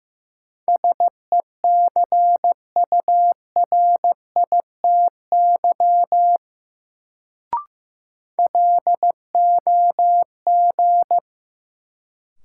Morse code, heard as a keyed tone, about 15 wpm, 700 Hz. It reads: SECURITY LOG
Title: Sentences from Top 1400 Words - New Word Drill Alt - 15wpm